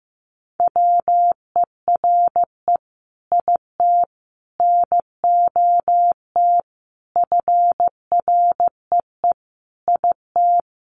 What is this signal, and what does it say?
Morse code, heard as a keyed tone, about 15 wpm, 700 Hz. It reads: WERE IT NOT FREE IT
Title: Sentences from Top 900 Words 15wpm